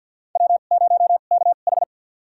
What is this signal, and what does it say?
Morse code, heard as a keyed tone, about 47 wpm, spaced as for 35 wpm, 700 Hz. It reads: W0KH
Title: USA Call Signs - 2x Character Spacing - 50wpm